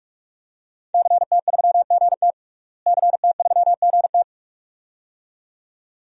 Morse code, heard as a keyed tone, about 48 wpm, 700 Hz. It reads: CT3GT CT3GT
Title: DXCC-100 Call Signs - 5 Characters - Repeated 2x - 45wpm